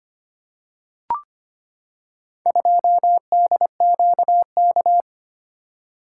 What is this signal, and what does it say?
Morse code, heard as a keyed tone, about 25 wpm, 700 Hz. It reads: 2DQK
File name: practice / 25wpm / Four Letter-Number Combinations - 25wpm